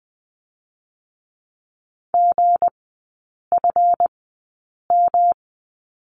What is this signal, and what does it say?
Morse code, heard as a keyed tone, about 20 wpm, spaced as for 8 wpm, 690 Hz. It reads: GFM
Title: 212 Lesson 34e - Number 8 - Three Characters 20wpm